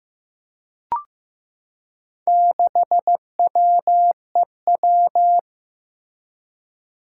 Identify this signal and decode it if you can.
Morse code, heard as a keyed tone, about 15 wpm, 700 Hz. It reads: 6WEW
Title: Four Challenging Letter-Number Combinations 15wpm